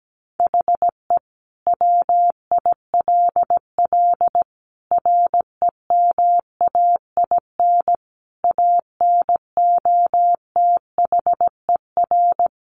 Morse code, heard as a keyed tone, about 17 wpm, 700 Hz. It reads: HE WILL REMAIN ANOTHER
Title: Sentences from Top 1200 Words 17wpm